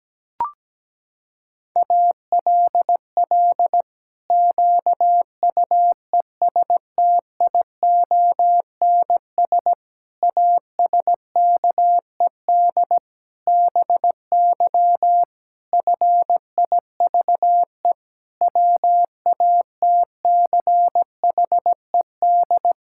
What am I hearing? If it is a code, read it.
Morse code, heard as a keyed tone, about 17 wpm, 700 Hz. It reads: ALL QUESTIONS ASKED BY FIVE WATCHED
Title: Pangram Challenge v2 - 17wpm